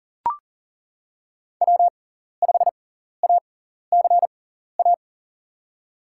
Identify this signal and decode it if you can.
Morse code, heard as a keyed tone, about 40 wpm, spaced as for 14 wpm, 700 Hz. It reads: W5ACA
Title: USA Call Signs - 6x Character Spacing - 40wpm